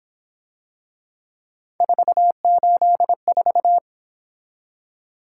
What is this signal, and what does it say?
Morse code, heard as a keyed tone, about 28 wpm, 700 Hz. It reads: 484
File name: Speed-Racing - Three Letter-Number Combinations - 22wpm